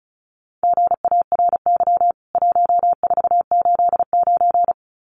Morse code, heard as a keyed tone, about 34 wpm, 700 Hz. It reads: GARY 1489
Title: CWT Exchange 35wpm